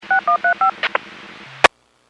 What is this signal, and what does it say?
DTMF, heard as telephone touch tones, tones of about 110 ms. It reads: 6135
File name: WPS / 00728 14-59 WPS 451.25 WPS